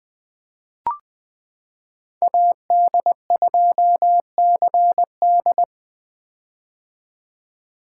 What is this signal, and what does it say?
Morse code, heard as a keyed tone, about 20 wpm, 700 Hz. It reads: AD2CD
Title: 2023 POTA Hunter Call Signs - 20wpm